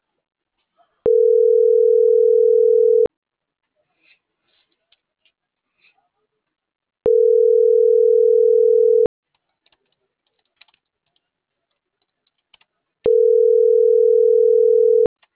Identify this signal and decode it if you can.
Ringback tone (North America), 440 + 480 Hz.